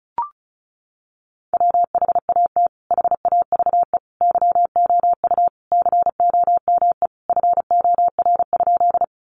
Morse code, heard as a keyed tone, about 34 wpm, 700 Hz. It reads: WHAT HAVE YOU COME FOR?
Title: Sentences from Top 100 Words 35wpm